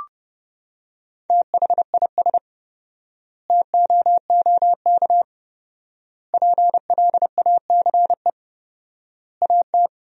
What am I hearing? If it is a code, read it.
Morse code, heard as a keyed tone, about 30 wpm, 700 Hz. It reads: THIS TOOK PLACE AT
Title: 4x Word-Spacing - Sentences from Top 300 Words - 30wpm